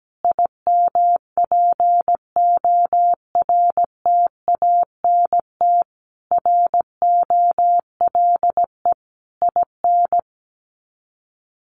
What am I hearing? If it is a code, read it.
Morse code, heard as a keyed tone, about 17 wpm, 700 Hz. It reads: IMPORTANT ROLE IN
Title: Sets of 5 Words Encore2 17wpm